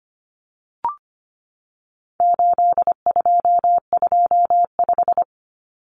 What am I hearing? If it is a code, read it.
Morse code, heard as a keyed tone, about 25 wpm, 700 Hz. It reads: 8225
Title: Common Cut Numbers - 4-Digit 25wpm